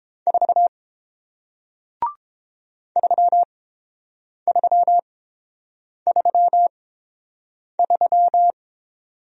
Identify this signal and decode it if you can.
Morse code, heard as a keyed tone, about 31 wpm, spaced as for 7 wpm, 700 Hz. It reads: 4 3333